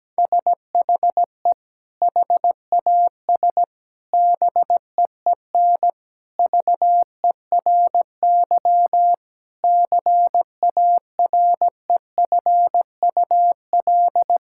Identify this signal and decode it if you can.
Morse code, heard as a keyed tone, about 17 wpm, 700 Hz. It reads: SHE HAS BEEN VERY CAREFUL
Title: Sentences from Top 1600 Words 17wpm